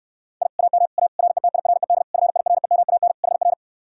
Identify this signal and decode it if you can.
Morse code, heard as a keyed tone, about 69 wpm, 700 Hz. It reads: I AM A RETIRED FIREMAN HR